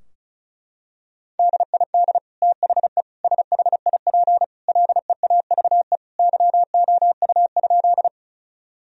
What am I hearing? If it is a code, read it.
Morse code, heard as a keyed tone, about 34 wpm, 700 Hz. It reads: DID THE SHIP LEAVE YOU?
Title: Sentences from Top 1000 Words - Review Part 2 - 35wpm